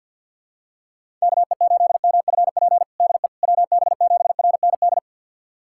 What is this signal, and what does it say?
Morse code, heard as a keyed tone, about 53 wpm, 700 Hz. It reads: KE8MFP DE WB7RND
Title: USA Callsign DE Callsign - 6 Characters - 50wpm